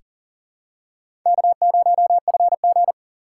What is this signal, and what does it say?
Morse code, heard as a keyed tone, about 40 wpm, 700 Hz. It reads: K0FG
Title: USA Call Signs - 4 Characters - 4x Character Spacing - 40wpm